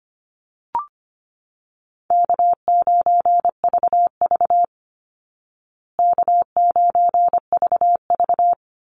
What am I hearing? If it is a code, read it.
Morse code, heard as a keyed tone, about 25 wpm, 700 Hz. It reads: K9VV K9VV
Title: Easy Call Signs Repeated 3x - 25wpm